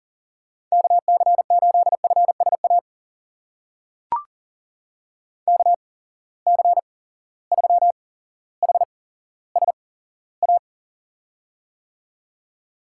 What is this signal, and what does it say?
Morse code, heard as a keyed tone, about 40 wpm, spaced as for 17 wpm, 700 Hz. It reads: KC8FSA KC3HSA
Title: USA Call Signs - 6 Characters - 8x Character Spacing - 40wpm